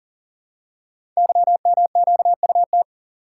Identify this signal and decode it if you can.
Morse code, heard as a keyed tone, about 38 wpm, 690 Hz. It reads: YMQUT
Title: Code Groups - 5 Letters - 40wpm